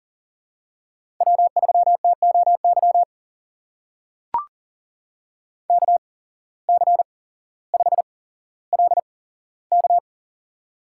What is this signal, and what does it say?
Morse code, heard as a keyed tone, about 40 wpm, spaced as for 17 wpm, 700 Hz. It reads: W3TOY KC5LK
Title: USA Call Signs - 8x Character Spacing - 40wpm